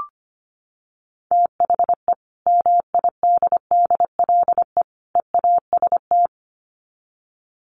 Morse code, heard as a keyed tone, about 25 wpm, 700 Hz. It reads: THE MIDDLE EAST
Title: Sets of 3 Words 25wpm